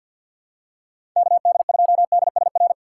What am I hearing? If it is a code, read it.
Morse code, heard as a keyed tone, about 52 wpm, 700 Hz. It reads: KD2DSR
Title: USA Call Signs - 6 Characters - 4x Character Spacing - 50wpm